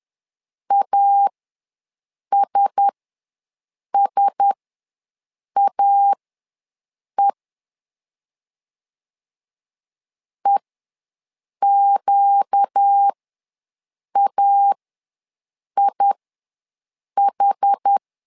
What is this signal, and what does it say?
Morse code, heard as a keyed tone, about 10 wpm, spaced as for 6 wpm, 800 Hz. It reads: ASSAE EQAIH
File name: LEZIONI / 4 Q1S ws 10wpm 2208